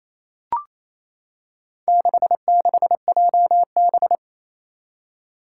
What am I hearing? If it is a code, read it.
Morse code, heard as a keyed tone, about 28 wpm, 700 Hz. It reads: 66JB
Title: Four Challenging Letter-Number Combinations 28wpm